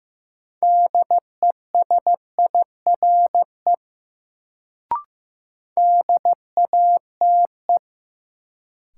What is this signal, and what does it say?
Morse code, heard as a keyed tone, about 15 wpm, 700 Hz. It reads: DESIRE DATE
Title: Sentences from Top 1100 Words - New Word Drill Alt - 15wpm